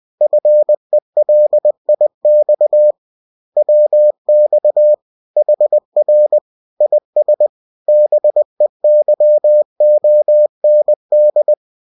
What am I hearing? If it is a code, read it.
Morse code, heard as a keyed tone, about 20 wpm, 600 Hz. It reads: FELIX WX HR IS BEYOND